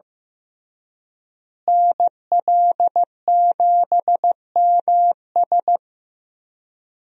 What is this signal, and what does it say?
Morse code, heard as a keyed tone, about 15 wpm, 700 Hz. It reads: NL7MS
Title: USA Call Signs - 5 Characters - Repeated 3x - 15wpm